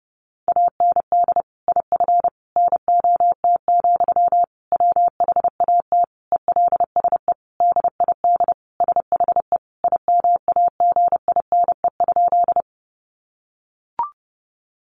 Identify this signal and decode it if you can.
Morse code, heard as a keyed tone, about 30 wpm, 700 Hz. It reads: AND IF NOT, WHAT ELSE DID SHE IMAGINE?
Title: Sentences from Top 1400 Words 30wpm